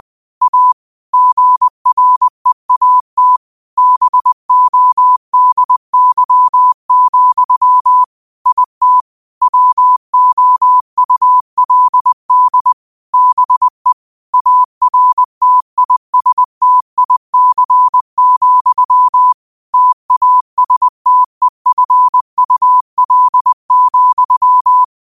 Morse code, heard as a keyed tone, about 20 wpm, 1000 Hz. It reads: A GREAT BODY, IT WOULD BE ARTISTIC, TASTEFUL,